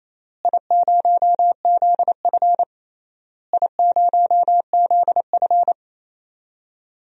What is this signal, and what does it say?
Morse code, heard as a keyed tone, about 28 wpm, 700 Hz. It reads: I0ZF I0ZF
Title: DXCC-100 Call Signs - 4 Characters - Repeated 3x - 28wpm